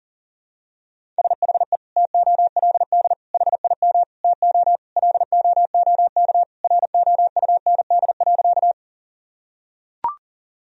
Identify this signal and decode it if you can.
Morse code, heard as a keyed tone, about 37 wpm, 700 Hz. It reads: SHE TOLD HIM TO LOOK ROUND.
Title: Sentences from Top 1000 Words - Review Part 1 - 40wpm